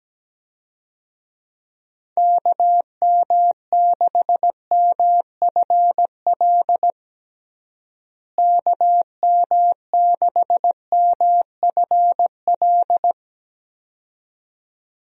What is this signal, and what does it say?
Morse code, heard as a keyed tone, about 17 wpm, 700 Hz. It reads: KM6MFL KM6MFL
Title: USA Call Signs - 6 Characters - Repeated 2x - 17wpm